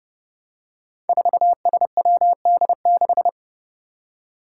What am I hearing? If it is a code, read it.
Morse code, heard as a keyed tone, about 30 wpm, 700 Hz. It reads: 4SWD6
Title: Five Challenging Letter-Number Combinations 30wpm